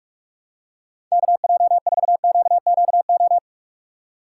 Morse code, heard as a keyed tone, about 46 wpm, 700 Hz. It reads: KJ4QQO
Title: USA Call Signs - 6 Characters - 2x Character Spacing - 45wpm